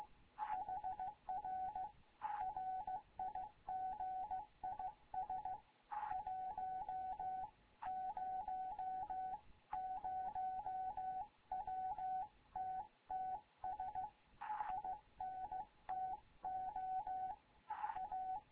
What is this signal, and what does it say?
Morse code, heard as a keyed tone, about 16 wpm, 750 Hz. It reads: HR RIG IS 100 WTTS INTO A
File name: Morze-HA2MN / hangp / 2 11 GYAKORLAT